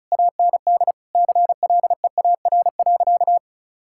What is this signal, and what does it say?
Morse code, heard as a keyed tone, about 36 wpm, 700 Hz. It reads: AND CLEAR.
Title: Sentences from Top 1000 Words - Review Part 6 - 35wpm